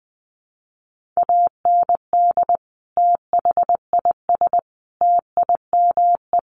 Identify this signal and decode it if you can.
Morse code, heard as a keyed tone, about 20 wpm, 700 Hz. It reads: AND THIS TIME
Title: Sentences from Top 1400 Words 20wpm